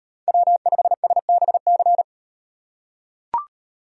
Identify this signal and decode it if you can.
Morse code, heard as a keyed tone, about 38 wpm, 700 Hz. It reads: W5SBC